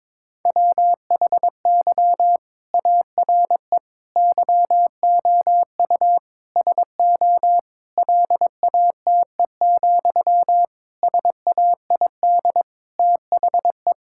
Morse code, heard as a keyed tone, about 22 wpm, 700 Hz. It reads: WHY ARE YOU SO LATE, SAID THE
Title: Sentences from Top 1000 Words - Review Part 6 - 22wpm